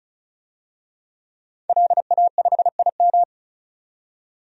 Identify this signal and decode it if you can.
Morse code, heard as a keyed tone, about 36 wpm, 700 Hz. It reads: LA5IM
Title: DXCC-100 Call Signs - 5 Characters - 35wpm